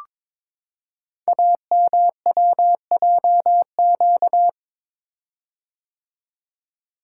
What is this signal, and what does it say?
Morse code, heard as a keyed tone, about 22 wpm, 700 Hz. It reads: AMWJQ
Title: Code Groups - 5 Letters - 22wpm